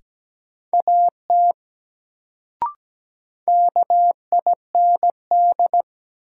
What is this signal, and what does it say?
Morse code, heard as a keyed tone, about 17 wpm, 700 Hz. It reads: AT KIND
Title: Sets of 2 Words - New Word Drill - 17wpm